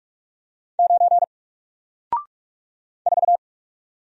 Morse code, heard as a keyed tone, about 45 wpm, 700 Hz. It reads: 9 4
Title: Single Character (Letter:Number:Punctuation:Prosign) Alt 45wpm